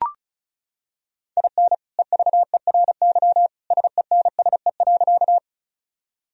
Morse code, heard as a keyed tone, about 36 wpm, 700 Hz. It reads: IN EVERY SENSE.
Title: Sentences from Top 1000 Words - Review Part 4 - 35wpm